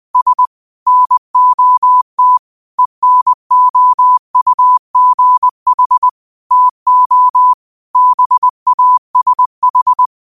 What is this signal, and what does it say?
Morse code, heard as a keyed tone, about 20 wpm, 1000 Hz. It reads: S NOT ENOUGH TO BASH
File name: qotdfiles / 20wpm